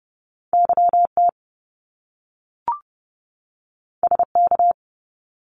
Morse code, heard as a keyed tone, about 30 wpm, 700 Hz. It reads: YT SK
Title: Canadian Provinces and Territories 30wpm